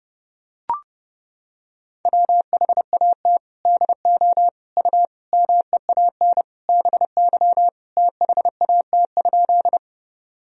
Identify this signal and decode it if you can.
Morse code, heard as a keyed tone, about 30 wpm, 700 Hz. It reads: WHAT DO U MEAN BY THAT?
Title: Sentences from Top 200 Words and Abbreviations 30wpm